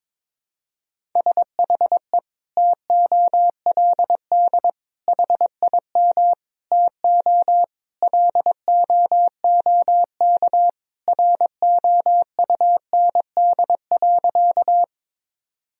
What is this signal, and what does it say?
Morse code, heard as a keyed tone, about 22 wpm, 700 Hz. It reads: SHE TOLD HIM TO LOOK ROUND.
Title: Sentences from Top 1000 Words - Review Part 1 - 22wpm